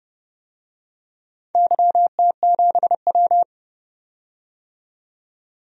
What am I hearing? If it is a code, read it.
Morse code, heard as a keyed tone, about 31 wpm, 700 Hz. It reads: YT7W
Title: DXCC-100 Call Signs - 4 Characters - 30wpm